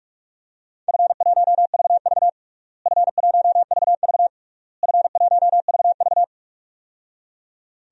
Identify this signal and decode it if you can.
Morse code, heard as a keyed tone, about 46 wpm, 700 Hz. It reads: F1VV F1VV F1VV
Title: DXCC-100 Call Signs - 4 Characters - Repeated 3x - 45wpm